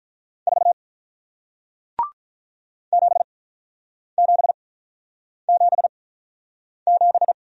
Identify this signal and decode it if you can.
Morse code, heard as a keyed tone, about 48 wpm, spaced as for 9 wpm, 700 Hz. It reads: 4 7777